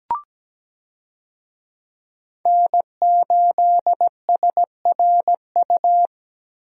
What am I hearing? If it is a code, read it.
Morse code, heard as a keyed tone, about 17 wpm, 700 Hz. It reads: N8SRU
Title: USA Call Signs - 5 Characters - Repeated 2x - 17wpm